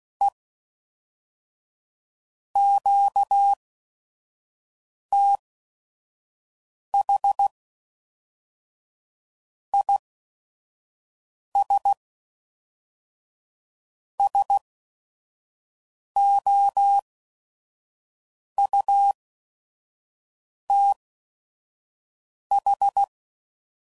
Morse code, heard as a keyed tone, about 16 wpm, spaced as for 5 wpm, 800 Hz. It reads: E QTH IS SOUTH
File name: CW Code Groups @5WPM / QSOs / kc7haa